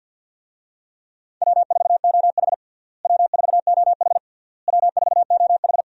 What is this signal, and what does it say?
Morse code, heard as a keyed tone, about 50 wpm, 700 Hz. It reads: W4OH W4OH W4OH
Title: USA Call Signs - 4 Characters - Repeated 3x - 50wpm